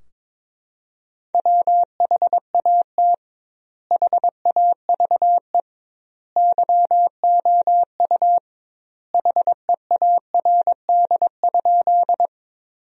Morse code, heard as a keyed tone, about 22 wpm, 700 Hz. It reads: WHAT HAVE YOU HEARD?
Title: 2x Word-Spacing - Sentences from Top 400 Words - 22wpm